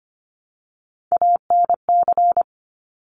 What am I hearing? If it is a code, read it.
Morse code, heard as a keyed tone, about 25 wpm, 700 Hz. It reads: ANC
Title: Top Three Letter Combinations in Words 25wpm